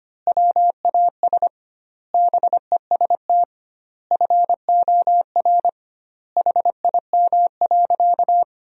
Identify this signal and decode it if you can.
Morse code, heard as a keyed tone, about 25 wpm, 700 Hz. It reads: WAS BEST FOR HIM.
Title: 2x Word-Spacing - Sentences from Top 500 Words - 25wpm